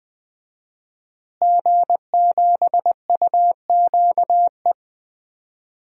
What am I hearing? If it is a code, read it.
Morse code, heard as a keyed tone, about 20 wpm, 700 Hz. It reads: G7UQE
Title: DXCC-100 Call Signs - 5 Characters - Repeated 2x - 20wpm